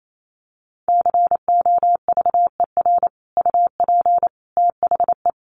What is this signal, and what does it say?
Morse code, heard as a keyed tone, about 28 wpm, 700 Hz. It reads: COVER UP THE
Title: Sentences from Top 1300 Words 28wpm